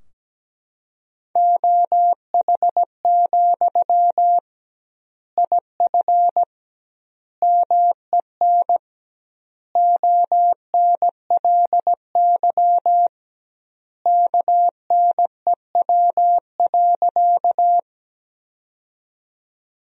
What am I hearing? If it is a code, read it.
Morse code, heard as a keyed tone, about 17 wpm, 700 Hz. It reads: OH, IF MEN ONLY KNEW.
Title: 2x Word-Spacing - Sentences from Top 500 Words - 17wpm